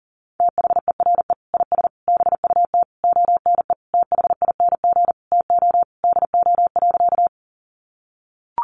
Morse code, heard as a keyed tone, about 40 wpm, 700 Hz. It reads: THERE IS BUT ONE THING TO DO.
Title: Sentences from Top 100 Words 40wpm